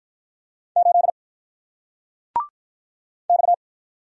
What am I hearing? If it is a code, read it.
Morse code, heard as a keyed tone, about 48 wpm, 700 Hz. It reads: 8 X